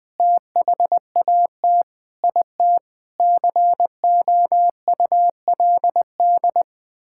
Morse code, heard as a keyed tone, about 20 wpm, 700 Hz. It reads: THAT IT COULD
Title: Sentences with 9-Word Phrases - 20wpm